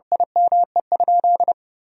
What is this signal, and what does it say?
Morse code, heard as a keyed tone, about 32 wpm, 700 Hz. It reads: IME?
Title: Sentences from Top 1000 Words - Review Part 3 - 30wpm